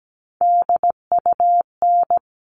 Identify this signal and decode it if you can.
Morse code, heard as a keyed tone, about 17 wpm, 700 Hz. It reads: DUN
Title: All Cut Numbers - 3-Digit 17wpm